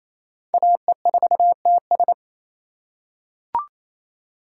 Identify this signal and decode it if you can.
Morse code, heard as a keyed tone, about 28 wpm, 700 Hz. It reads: AE4TS
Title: USA Call Signs - 5 Characters - 2x Character Spacing - 28wpm